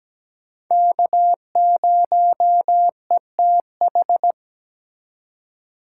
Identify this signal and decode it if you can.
Morse code, heard as a keyed tone, about 20 wpm, 700 Hz. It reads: K0ETH
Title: Speed Racing - USA 1x3 Call Signs 17wpm